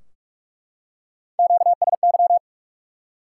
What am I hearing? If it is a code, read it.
Morse code, heard as a keyed tone, about 44 wpm, 700 Hz. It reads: QSY
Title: Top 180 QSO Elements 45wpm